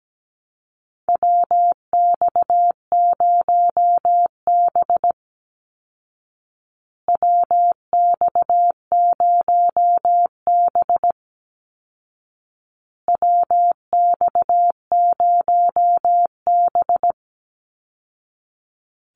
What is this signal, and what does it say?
Morse code, heard as a keyed tone, about 17 wpm, 700 Hz. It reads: WX0B WX0B WX0B
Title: Call Signs Repeated 2x - 17wpm